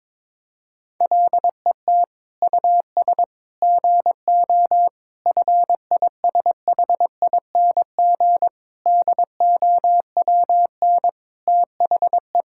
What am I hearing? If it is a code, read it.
Morse code, heard as a keyed tone, about 22 wpm, 700 Hz. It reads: LET US GO FISHING DOWN THE
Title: Sentences from Top 1000 Words - Review Part 5 - 22wpm